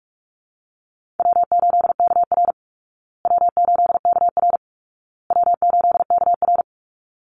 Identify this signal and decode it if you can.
Morse code, heard as a keyed tone, about 44 wpm, 700 Hz. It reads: W8KR W8KR W8KR
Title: Easy Call Signs Repeated 3x - 45wpm